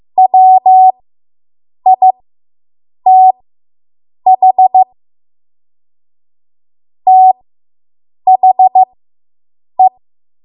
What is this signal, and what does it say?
Morse code, heard as a keyed tone, about 15 wpm, spaced as for 7 wpm, 750 Hz. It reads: WITH THE